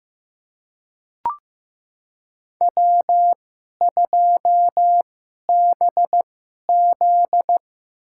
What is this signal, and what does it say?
Morse code, heard as a keyed tone, about 16 wpm, spaced as for 11 wpm, 700 Hz. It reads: W2BZ
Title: USA Call Signs - 4 Characters - 2x Character Spacing - 15wpm